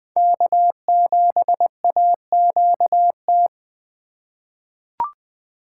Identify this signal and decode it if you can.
Morse code, heard as a keyed tone, about 20 wpm, 700 Hz. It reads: K7AQT
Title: USA Call Signs - 5 Characters - 2x Character Spacing - 20wpm